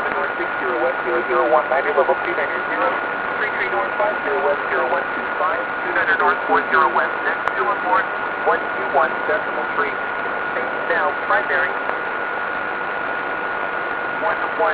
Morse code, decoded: none

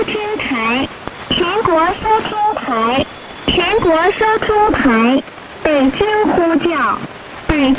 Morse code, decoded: none

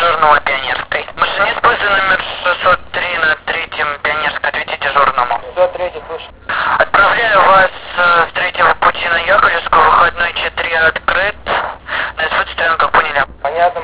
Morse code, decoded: none

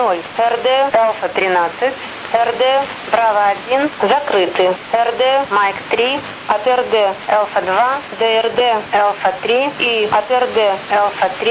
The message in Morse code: none

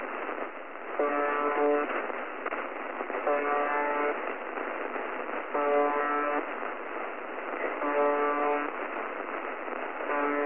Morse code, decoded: none

